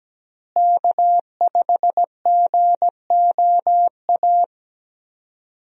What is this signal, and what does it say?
Morse code, heard as a keyed tone, about 17 wpm, 700 Hz. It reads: K5GOA
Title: USA Call Signs - 5 Characters - Repeated 3x - 17wpm